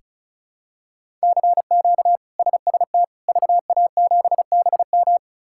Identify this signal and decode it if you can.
Morse code, heard as a keyed tone, about 36 wpm, 700 Hz. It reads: CQ SST VA7BM
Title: K1USN SST QSO Practice - 35wpm